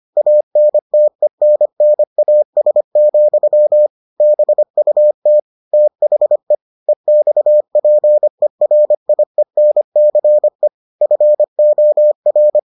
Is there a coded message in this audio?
Yes: Morse code from 0.2 to 12.6 s, heard as a keyed tone, about 25 wpm, 600 Hz. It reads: ANTENNAS, BUT THE EXPERIENCE FOR